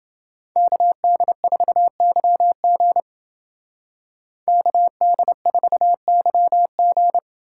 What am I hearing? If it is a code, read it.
Morse code, heard as a keyed tone, about 29 wpm, 700 Hz. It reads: KD4YG KD4YG